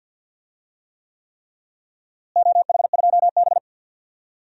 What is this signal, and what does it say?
Morse code, heard as a keyed tone, about 47 wpm, 700 Hz. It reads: OH2B
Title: DXCC-100 Call Signs - 4 Characters - Repeated 3x - 50wpm